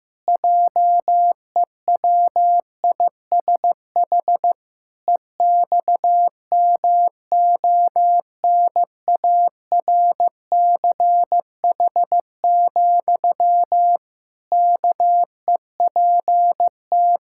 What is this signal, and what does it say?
Morse code, heard as a keyed tone, about 15 wpm, 700 Hz. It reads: JEWISH EXMONARCH, KEPT
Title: Pangram Challenge v2 - 15wpm